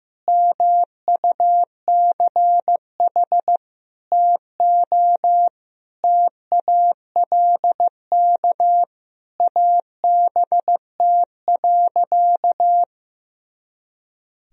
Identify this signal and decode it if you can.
Morse code, heard as a keyed tone, about 15 wpm, 700 Hz. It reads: MUCH TO TALK ABT.
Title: Sentences from Top 400 Words and Abbreviations 15wpm